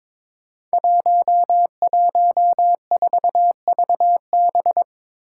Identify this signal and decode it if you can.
Morse code, heard as a keyed tone, about 22 wpm, 700 Hz. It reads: 114VB
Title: Five Challenging Letter-Number Combinations 22wpm